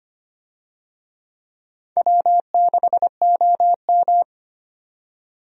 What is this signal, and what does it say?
Morse code, heard as a keyed tone, about 25 wpm, 700 Hz. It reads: W6OM
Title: USA Call Signs - 4 Characters - 25wpm